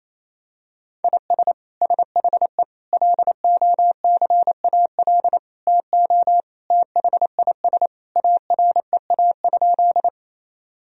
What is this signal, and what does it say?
Morse code, heard as a keyed tone, about 28 wpm, 700 Hz. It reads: IS SHE LOCAL TO THIS AREA?
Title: Sentences from Top 1000 Words - Review Part 3 - 28wpm